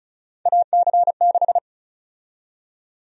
Morse code, heard as a keyed tone, about 36 wpm, 700 Hz. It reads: AC6
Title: USA Call Sign Prefixes - 35wpm